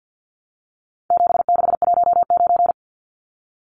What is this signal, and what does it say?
Morse code, heard as a keyed tone, about 50 wpm, 700 Hz. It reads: 7619
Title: Numbers - 4-Digit 50wpm